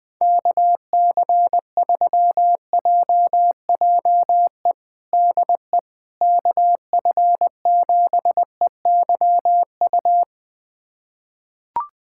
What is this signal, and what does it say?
Morse code, heard as a keyed tone, about 20 wpm, 700 Hz. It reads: KC3JJE DE KF7EYU
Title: USA Callsign DE Callsign 20wpm